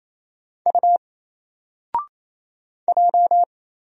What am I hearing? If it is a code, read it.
Morse code, heard as a keyed tone, about 27 wpm, 700 Hz. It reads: U J
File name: Single-Letter-Number-28wpm